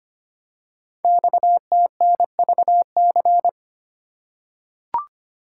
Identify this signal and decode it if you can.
Morse code, heard as a keyed tone, about 25 wpm, 700 Hz. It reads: XTNVC
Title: Code Groups - 5 Letter:Numbers - 25wpm